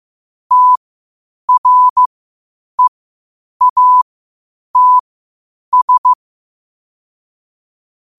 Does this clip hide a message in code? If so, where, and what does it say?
Yes: Morse code from 0.5 to 6.1 s, heard as a keyed tone, about 15 wpm, spaced as for 8 wpm, 1000 Hz. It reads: TREATS